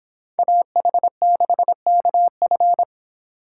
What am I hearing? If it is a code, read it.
Morse code, heard as a keyed tone, about 26 wpm, 700 Hz. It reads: AH6KF